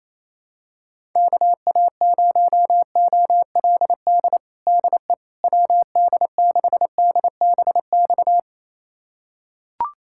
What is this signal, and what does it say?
Morse code, heard as a keyed tone, about 28 wpm, 700 Hz. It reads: KA0OLD DE WD6DBX
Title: USA Callsign DE Callsign - 6 Characters - 28wpm